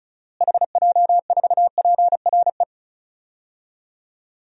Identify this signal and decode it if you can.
Morse code, heard as a keyed tone, about 36 wpm, 700 Hz. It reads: HJ4PRE
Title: DXCC-100 Call Signs - 6 Characters - Repeated 2x - 35wpm